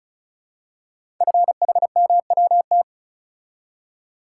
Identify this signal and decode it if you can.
Morse code, heard as a keyed tone, about 36 wpm, 690 Hz. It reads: FHMWT
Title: Code Groups - 5 Letters - 35wpm